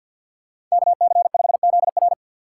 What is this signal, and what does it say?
Morse code, heard as a keyed tone, about 53 wpm, 700 Hz. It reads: KC5ZR
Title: USA Call Signs - 6x Character Spacing - 50wpm